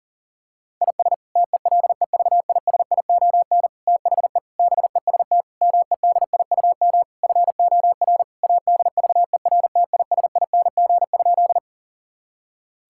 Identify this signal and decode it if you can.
Morse code, heard as a keyed tone, about 40 wpm, 700 Hz. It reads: IS TELEVISION THE BEST MEDIUM FOR ADVERTISING?
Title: Sentences from Top 1000 Words - Review Part 6 - 40wpm